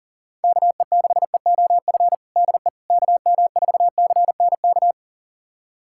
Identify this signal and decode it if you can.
Morse code, heard as a keyed tone, about 37 wpm, 700 Hz. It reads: KE6EOF DE KM4CNK
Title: USA Callsign DE Callsign - 6 Characters - 40wpm